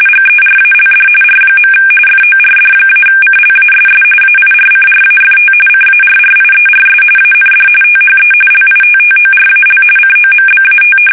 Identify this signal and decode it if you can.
RTTY, about 45.45 baud. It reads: GERALD R. FORD-CLASS AIRCRAFT CARRIERS AND THE LITTORAL COMBAT
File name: fsk850 / usnavy5